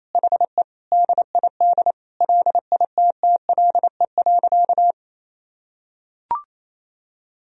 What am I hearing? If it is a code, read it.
Morse code, heard as a keyed tone, about 28 wpm, 700 Hz. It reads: HE DID LITTLE.
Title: Sentences from Top 1000 Words - Review Part 1 - 28wpm